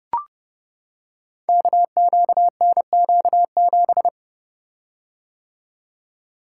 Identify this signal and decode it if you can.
Morse code, heard as a keyed tone, about 30 wpm, 700 Hz. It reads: KQNQ7
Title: Code Groups - 5 Letter:Numbers - 30wpm